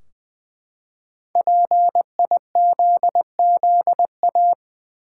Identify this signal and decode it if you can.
Morse code, heard as a keyed tone, about 20 wpm, 700 Hz. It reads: PIZZA
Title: Five Letter Words - 20wpm